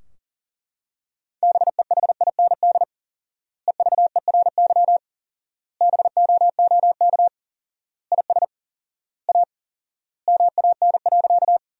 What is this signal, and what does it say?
Morse code, heard as a keyed tone, about 40 wpm, 700 Hz. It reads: BEHIND EVERY BOOK IS A MAN.
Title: 4x Word-Spacing - Sentences from Top 500 Words - 40wpm